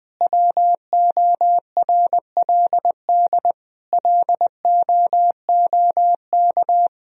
Morse code, heard as a keyed tone, about 19 wpm, 700 Hz. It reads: WORLD LOOK
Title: Sentences from Top 1000 Words - Review Part 2 - 20wpm